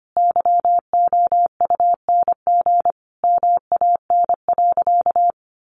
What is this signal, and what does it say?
Morse code, heard as a keyed tone, about 25 wpm, 700 Hz. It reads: YOUNG MAN.
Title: Sentences from Top 1300 Words 25wpm